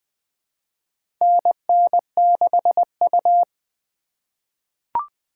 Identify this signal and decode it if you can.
Morse code, heard as a keyed tone, about 20 wpm, 700 Hz. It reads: NN6U
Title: USA Call Signs - 4 Characters - 3x Character Spacing - 20wpm